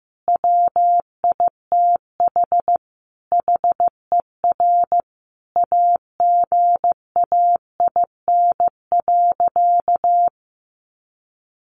Morse code, heard as a keyed tone, about 15 wpm, 700 Hz. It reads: WITH HER AGAIN.
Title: Sentences from Top 1500 Words 15wpm